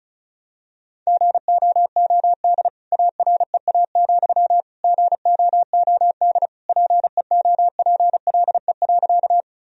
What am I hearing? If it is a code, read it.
Morse code, heard as a keyed tone, about 34 wpm, 700 Hz. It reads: GOOD AREA, GOOD PEOPLE.
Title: Sentences from Top 1000 Words - Review Part 2 - 35wpm